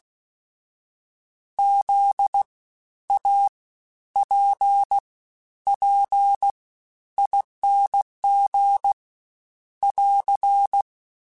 Morse code, heard as a keyed tone, about 16 wpm, spaced as for 10 wpm, 800 Hz. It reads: ZAPPING <AR>
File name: Special INGs-mp3s / Special INGs-mp3s / SlowToFaster ing @15